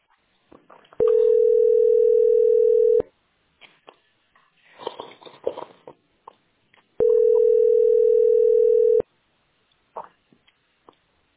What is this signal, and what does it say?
Ringback tone (North America), 440 + 480 Hz.
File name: d42a6142-74fe-4f3c-8c65